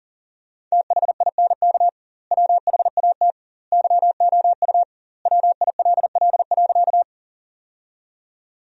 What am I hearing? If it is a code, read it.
Morse code, heard as a keyed tone, about 37 wpm, 700 Hz. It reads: THINK WHAT YOU WILL.
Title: 2x Word-Spacing - Sentences from Top 200 Words - 40wpm